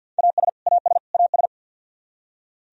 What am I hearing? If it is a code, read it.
Morse code, heard as a keyed tone, about 48 wpm, 700 Hz. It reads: AS AS AS
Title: Top 100 Words Repeated 50wpm